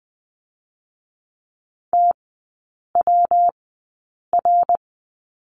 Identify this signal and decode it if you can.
Morse code, heard as a keyed tone, about 20 wpm, 700 Hz. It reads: T W R
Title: 114 Lesson 19e - Letter w - Three Characters 20wpm